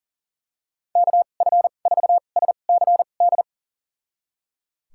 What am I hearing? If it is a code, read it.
Morse code, heard as a keyed tone, about 37 wpm, spaced as for 28 wpm, 700 Hz. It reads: KF4SCD
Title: USA Call Signs - 6 Characters - 2x Character Spacing - 40wpm